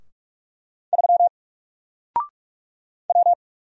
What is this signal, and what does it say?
Morse code, heard as a keyed tone, about 46 wpm, 700 Hz. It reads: 3 W
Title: Single Character (Letter:Number:Punctuation:Prosign) Alt 45wpm